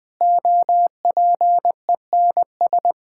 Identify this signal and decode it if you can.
Morse code, heard as a keyed tone, about 20 wpm, 700 Hz. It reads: OPENS
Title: Difficult Sentences with Hints - 20wpm